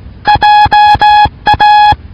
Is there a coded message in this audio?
Yes: Morse code from 0.3 to 1.9 s, heard as a keyed tone, about 18 wpm, 840 Hz. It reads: JA